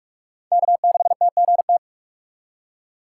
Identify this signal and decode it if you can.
Morse code, heard as a keyed tone, about 46 wpm, 700 Hz. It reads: K6TGT
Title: USA Call Signs - 5 Characters - 3x Character Spacing - 45wpm